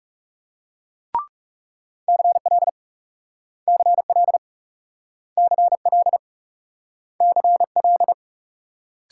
Morse code, heard as a keyed tone, about 38 wpm, 700 Hz. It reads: CL CL CL CL